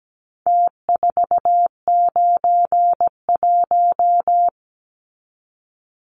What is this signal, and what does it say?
Morse code, heard as a keyed tone, about 17 wpm, 700 Hz. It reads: T491
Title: Numbers - Contest Style - 4-Digit 17wpm